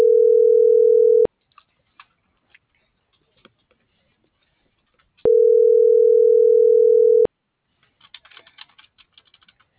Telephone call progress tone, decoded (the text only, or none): ringback tone